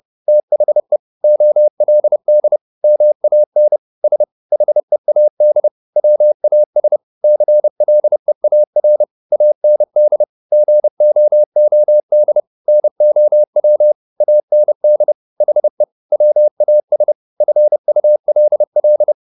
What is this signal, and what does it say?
Morse code, heard as a keyed tone, about 30 wpm, 600 Hz. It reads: THE OLD MAN S HEAD WAS CLEAR AND GOOD NOW AND HE WAS FULL